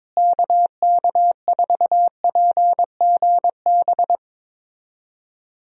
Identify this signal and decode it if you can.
Morse code, heard as a keyed tone, about 22 wpm, 700 Hz. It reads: KK4PGB
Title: USA Call Signs - 6 Characters - Repeated 2x - 22wpm